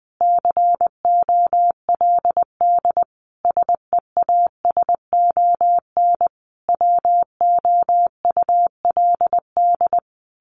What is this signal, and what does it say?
Morse code, heard as a keyed tone, about 20 wpm, 700 Hz. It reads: COLD SEASON WOULD
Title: Sentences from Top 1400 Words 20wpm